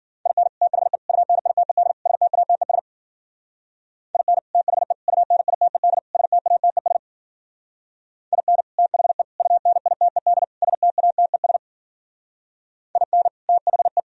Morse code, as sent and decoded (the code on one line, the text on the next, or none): .. -. / - .... / ..- -. .. - . -.. / ... - .- - . ... / .. -. / - .... / ..- -. .. - . -.. / ... - .- - . ... / .. -. / - .... / ..- -. .. - . -.. / ... - .- - . ... / .. -. / - .... .
IN THE UNITED STATES IN THE UNITED STATES IN THE UNITED STATES IN THE